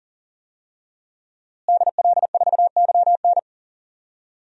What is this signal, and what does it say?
Morse code, heard as a keyed tone, about 40 wpm, 700 Hz. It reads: DL4YN
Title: DXCC-100 Call Signs - 5 Characters - Repeated 2x - 40wpm